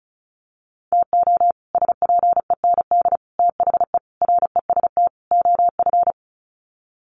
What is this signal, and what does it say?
Morse code, heard as a keyed tone, about 34 wpm, 700 Hz. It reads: TO SPEND THE REST OF
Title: Sets of 5 Words Encore2 35wpm